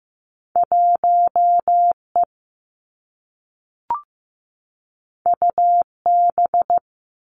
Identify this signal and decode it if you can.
Morse code, heard as a keyed tone, about 15 wpm, 700 Hz. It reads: 1 E U B
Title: Two Challenging Letter-Number Combinations 15wpm